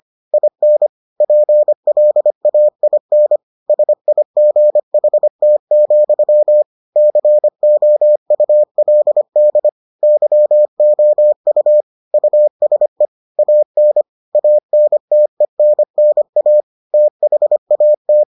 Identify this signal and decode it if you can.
Morse code, heard as a keyed tone, about 25 wpm, 600 Hz. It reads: IN PLAIN SIGHT, COULD YOU USE AN ANTENNA THAT